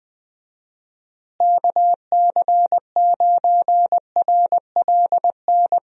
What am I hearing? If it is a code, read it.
Morse code, heard as a keyed tone, about 20 wpm, 700 Hz. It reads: KC9RLN
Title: USA Call Signs - 6 Characters - Repeated 2x - 20wpm